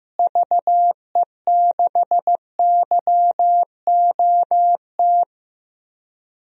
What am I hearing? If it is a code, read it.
Morse code, heard as a keyed tone, about 15 wpm, 700 Hz. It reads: VE6YOT
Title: DXCC-100 Call Signs - 6 Characters - Repeated 3x - 15wpm